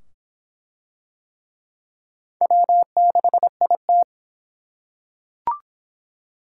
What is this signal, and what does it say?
Morse code, heard as a keyed tone, about 22 wpm, 700 Hz. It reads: W6IT